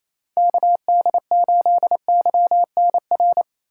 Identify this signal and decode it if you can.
Morse code, heard as a keyed tone, about 28 wpm, 700 Hz. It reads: KD8YNR
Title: USA Callsign DE Callsign 28wpm